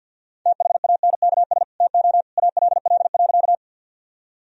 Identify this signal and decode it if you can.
Morse code, heard as a keyed tone, about 50 wpm, 700 Hz. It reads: THANKS TO ALL.
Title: Sentences from Top 1000 Words - Review Part 3 - 50wpm